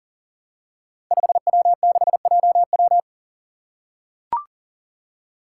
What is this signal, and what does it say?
Morse code, heard as a keyed tone, about 40 wpm, 700 Hz. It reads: 5W6JW
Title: Five Challenging Letter-Number Combinations 40wpm